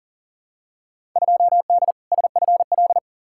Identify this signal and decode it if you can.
Morse code, heard as a keyed tone, about 40 wpm, 700 Hz. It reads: 2D SFL
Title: Field Day Exchanges Encore - 40wpm